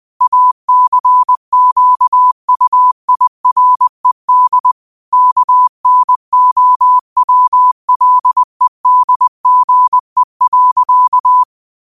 Morse code, heard as a keyed tone, about 20 wpm, 1000 Hz. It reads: ACQUIRED KNOWLEDGE.
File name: qotdfiles / 20wpm